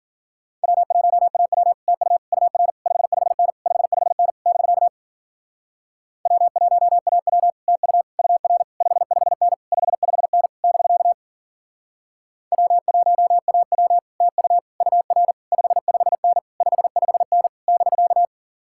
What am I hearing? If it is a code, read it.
Morse code, heard as a keyed tone, about 49 wpm, 700 Hz. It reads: W1AW TU UR 55N 55N <BK> W1AW TU UR 55N 55N <BK> W1AW TU UR 55N 55N <BK>
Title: Speed-Racing Common QSO Phrases Encore - 40wpm